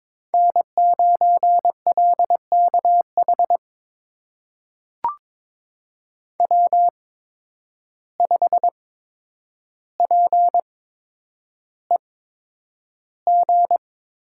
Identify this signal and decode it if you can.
Morse code, heard as a keyed tone, about 22 wpm, spaced as for 9 wpm, 700 Hz. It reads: N9LKH W5PEG
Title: USA Call Signs - 5 Characters - 8x Character Spacing - 22wpm